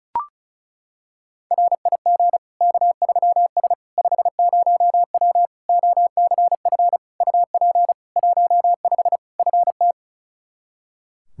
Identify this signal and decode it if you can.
Morse code, heard as a keyed tone, about 34 wpm, 700 Hz. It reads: RIG K3S 50W OCF UP 15 FT
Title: Practice QSO - Exchange 2 - Info - 35wpm